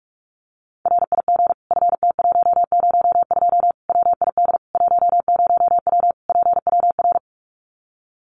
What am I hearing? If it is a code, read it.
Morse code, heard as a keyed tone, about 44 wpm, 700 Hz. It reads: RIG FT102 WID 10W PWR
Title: Common QSO Phrases 45wpm